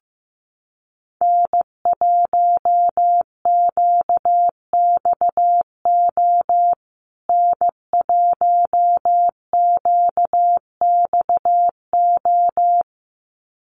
Morse code, heard as a keyed tone, about 15 wpm, 700 Hz. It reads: N1QXO N1QXO
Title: USA Call Signs Repeated 2x 15wpm